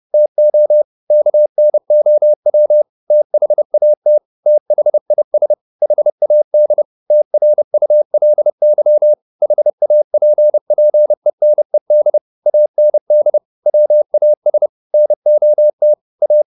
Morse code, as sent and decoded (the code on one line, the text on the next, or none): - --- / -.- -. --- .-- / - .... .- - / - .... .. ... / .... .- -.. / - .-. ..- .-.. -.-- / .... .- .--. .--. -. -.. / .- -. -.. / .-- .- ... / -. --- - / .-
TO KNOW THAT THIS HAD TRULY HAPPENED AND WAS NOT A